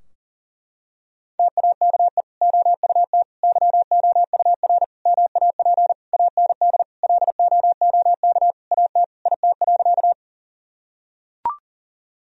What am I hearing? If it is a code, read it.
Morse code, heard as a keyed tone, about 37 wpm, 700 Hz. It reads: TAKE OUT YOUR MAP AND LOOK AT IT.
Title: Sentences from Top 1000 Words - Review Part 2 - 40wpm